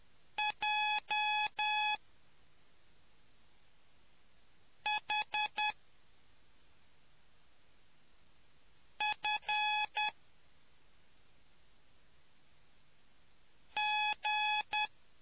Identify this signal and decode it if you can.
Morse code, heard as a keyed tone, about 10 wpm, spaced as for 2 wpm, 850 Hz. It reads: JHFG